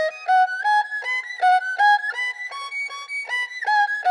Emergency Alert System (SAME) tones, heard as harsh buzzing bursts. It is heard from 1.0 to 3.6 s.